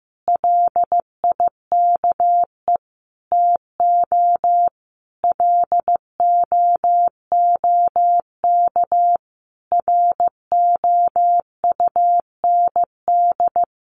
Morse code, heard as a keyed tone, about 15 wpm, 700 Hz. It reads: LIKE TO LOOK ROUND
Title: Sentences from Top 200 Words 15wpm